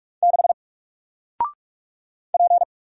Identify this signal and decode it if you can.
Morse code, heard as a keyed tone, about 43 wpm, 700 Hz. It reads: 6 P
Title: Single Character (Letter:Number:Punctuation:Prosign) Alt 45wpm